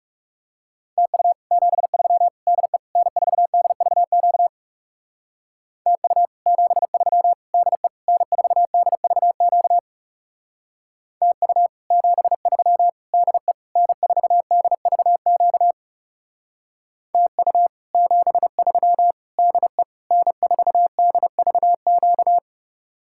Morse code, heard as a keyed tone, about 40 wpm, 700 Hz. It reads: TU 73 DE N4DVQ TU 73 DE N4DVQ TU 73 DE N4DVQ TU 73 DE N4DVQ